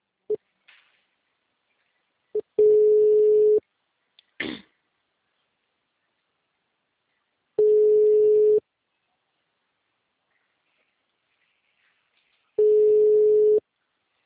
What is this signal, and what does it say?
Ringback tone (Europe, 425 Hz).